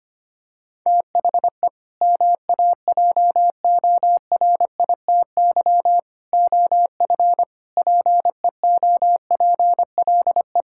Morse code, heard as a keyed tone, about 25 wpm, 700 Hz. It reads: THE MAJORITY OF PEOPLE
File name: practice / 25wpm / Sentences with 3-Word Phrases - 25wpm